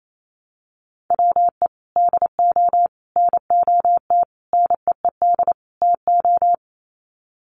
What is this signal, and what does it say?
Morse code, heard as a keyed tone, about 28 wpm, 700 Hz. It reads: WE DO NOT NEED TO
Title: Sets of 5 Words Encore 28wpm